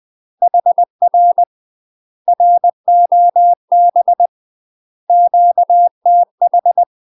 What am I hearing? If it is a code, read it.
Morse code, heard as a keyed tone, about 20 wpm, 700 Hz. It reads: HR ROB QTH
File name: UK MASTER QSOs TRAINING/20WPM